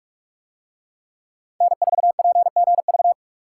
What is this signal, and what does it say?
Morse code, heard as a keyed tone, about 44 wpm, 700 Hz. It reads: N4PGV